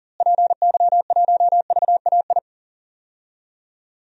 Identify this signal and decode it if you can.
Morse code, heard as a keyed tone, about 40 wpm, 700 Hz. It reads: PY1VAI
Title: DXCC-100 Call Signs - 6 Characters - Repeated 2x - 40wpm